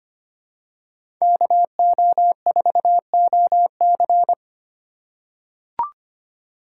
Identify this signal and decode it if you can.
Morse code, heard as a keyed tone, about 25 wpm, 700 Hz. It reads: KO4OC
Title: USA Call Signs - 5 Characters - 8x Character Spacing - 25wpm